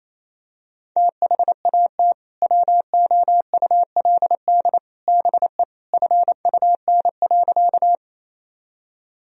Morse code, heard as a keyed tone, about 28 wpm, 700 Hz. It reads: THAT WOULD BE FUN.
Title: Sentences from Top 1000 Words - Review Part 4 - 28wpm